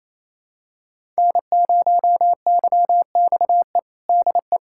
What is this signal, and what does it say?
Morse code, heard as a keyed tone, about 27 wpm, 700 Hz. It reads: N0YXE DE
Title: USA Callsign DE Callsign 28wpm